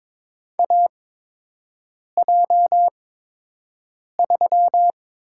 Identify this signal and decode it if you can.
Morse code, heard as a keyed tone, about 22 wpm, spaced as for 6 wpm, 700 Hz. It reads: AJ3
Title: USA Call Signs - 5 Characters - 8x Character Spacing - 22wpm